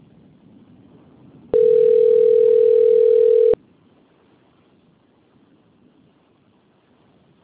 Ringback tone (North America), 440 + 480 Hz.